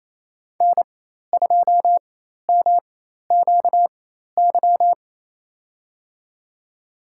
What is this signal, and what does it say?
Morse code, heard as a keyed tone, about 27 wpm, spaced as for 13 wpm, 700 Hz. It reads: N2MQY
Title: USA Call Signs - 5 Characters - 4x Character Spacing - 28wpm